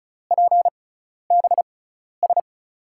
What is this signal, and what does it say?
Morse code, heard as a keyed tone, about 34 wpm, spaced as for 12 wpm, 700 Hz. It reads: PBS
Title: USA Call Signs - 5 Characters - 6x Character Spacing - 35wpm